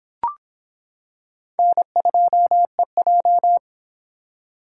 Morse code, heard as a keyed tone, about 26 wpm, 700 Hz. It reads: N2EJ